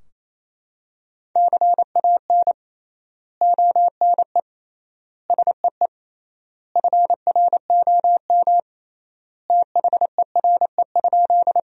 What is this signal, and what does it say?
Morse code, heard as a keyed tone, about 28 wpm, 700 Hz. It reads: CAN ONE SEE FROM THERE?
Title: 3x Word-Spacing - Sentences from Top 100 Words - 28wpm